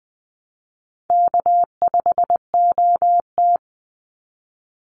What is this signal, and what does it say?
Morse code, heard as a keyed tone, about 20 wpm, 700 Hz. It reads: K5OT
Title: Easy Call Signs - 20wpm